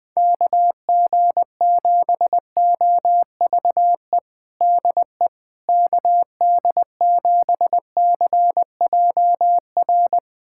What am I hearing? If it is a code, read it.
Morse code, heard as a keyed tone, about 21 wpm, 700 Hz. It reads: KG7OVE DE KD7CJR
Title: USA Callsign DE Callsign 20wpm